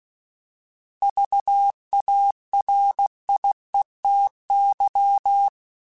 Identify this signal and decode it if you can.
Morse code, heard as a keyed tone, about 16 wpm, 800 Hz. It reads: VARIETY